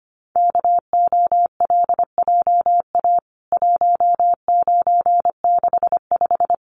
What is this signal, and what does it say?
Morse code, heard as a keyed tone, about 25 wpm, 700 Hz. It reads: KOLJA 1965
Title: CWT Exchange 25wpm